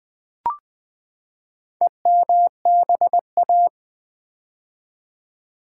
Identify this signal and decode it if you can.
Morse code, heard as a keyed tone, about 20 wpm, 700 Hz. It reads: EMBA
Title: Build A Word - 4 Letters At A Time - 11-Letter Words - 20wpm